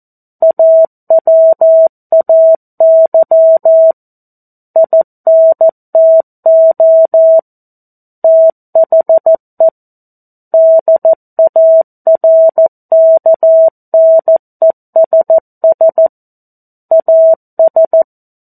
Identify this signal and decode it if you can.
Morse code, heard as a keyed tone, about 14 wpm, 650 Hz. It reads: AWAY INTO THE DARKNESS AS
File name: 02-Chapter-02 14 WPM